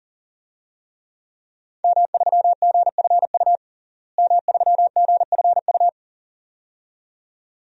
Morse code, heard as a keyed tone, about 40 wpm, 700 Hz. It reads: M3GFU M3GFU
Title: DXCC-100 Call Signs - 5 Characters - Repeated 2x - 40wpm